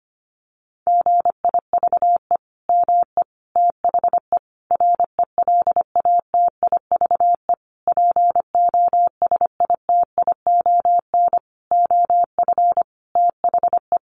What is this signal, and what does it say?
Morse code, heard as a keyed tone, about 25 wpm, 700 Hz. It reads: GIVE ME THE RELATIVE POSITION OF THE
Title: Sentences from Top 1400 Words 25wpm